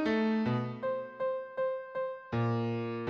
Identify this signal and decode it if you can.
There is no code.